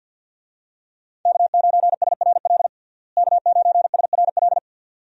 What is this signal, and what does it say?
Morse code, heard as a keyed tone, about 51 wpm, 700 Hz. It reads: K9SRL K9SRL
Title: USA Call Signs - 5 Characters - Repeated 2x - 50wpm